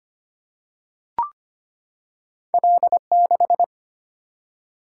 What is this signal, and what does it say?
Morse code, heard as a keyed tone, about 25 wpm, 700 Hz. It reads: L6
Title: Single-Letter Single-Number 25wpm